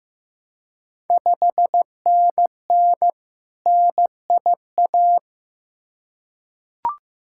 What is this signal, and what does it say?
Morse code, heard as a keyed tone, about 15 wpm, 700 Hz. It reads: 5NN NIA
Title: Ontario QSO Party Exchange 15wpm